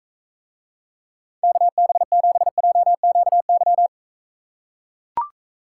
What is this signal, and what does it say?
Morse code, heard as a keyed tone, about 44 wpm, 700 Hz. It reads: KB7JQY